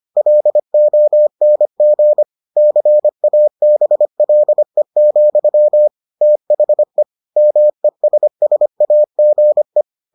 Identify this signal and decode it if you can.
Morse code, heard as a keyed tone, about 25 wpm, 600 Hz. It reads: LONG CABLE, THE MESSAGE